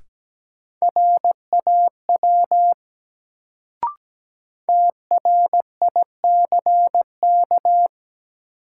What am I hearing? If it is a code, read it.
Morse code, heard as a keyed tone, about 17 wpm, 700 Hz. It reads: RAW TRICK
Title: Sentences from Top 1500 Words - New Word Drill - 17wpm